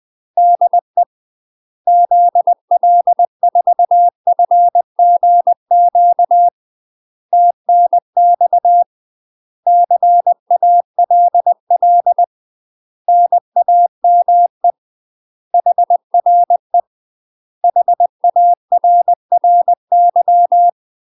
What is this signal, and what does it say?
Morse code, heard as a keyed tone, about 20 wpm, 700 Hz. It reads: DE ZL4FGQ TNX CALL NAME HRE HARRY
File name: UK MASTER QSOs TRAINING / 20WPM / UK MASTER QSOs 0029